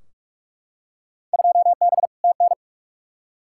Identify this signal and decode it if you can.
Morse code, heard as a keyed tone, about 44 wpm, 700 Hz. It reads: 2B TN
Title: Field Day Exchanges Encore - 45wpm